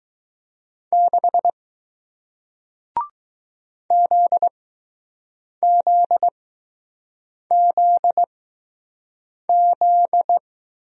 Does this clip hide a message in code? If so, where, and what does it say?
Yes: Morse code from 0.9 to 10.4 s, heard as a keyed tone, about 20 wpm, spaced as for 6 wpm, 700 Hz. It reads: 6 ZZZZ